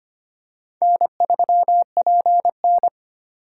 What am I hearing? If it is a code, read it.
Morse code, heard as a keyed tone, about 25 wpm, 700 Hz. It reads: N3PN
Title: USA Call Signs - 4 Characters - 2x Character Spacing - 25wpm